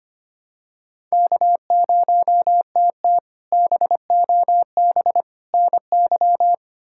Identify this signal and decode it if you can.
Morse code, heard as a keyed tone, about 25 wpm, 700 Hz. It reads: K0TT BOB NY
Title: K1USN SST QSO Practice - 25wpm